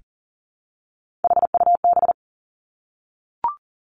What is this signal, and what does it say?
Morse code, heard as a keyed tone, about 41 wpm, 700 Hz. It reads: HUB
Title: Three Challenging Letter-Number Combinations 40wpm